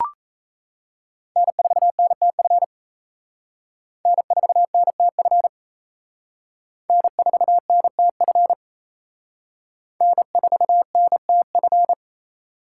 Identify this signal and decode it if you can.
Morse code, heard as a keyed tone, about 35 wpm, 700 Hz. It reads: N4NTF N4NTF N4NTF N4NTF
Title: Speed Racing - USA 1x3 Call Signs 28wpm